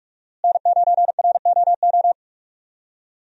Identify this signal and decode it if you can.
Morse code, heard as a keyed tone, about 46 wpm, 700 Hz. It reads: N9ROO
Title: USA Call Signs - 5 Characters - Repeated 3x - 45wpm